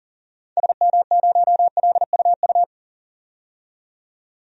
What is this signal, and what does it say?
Morse code, heard as a keyed tone, about 40 wpm, 700 Hz. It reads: SM0LUU
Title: DXCC-100 Call Signs - 6 Characters - 40wpm